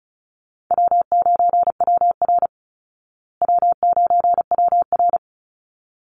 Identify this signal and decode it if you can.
Morse code, heard as a keyed tone, about 34 wpm, 700 Hz. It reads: W9WR W9WR
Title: Easy Call Signs Repeated 2x - 35wpm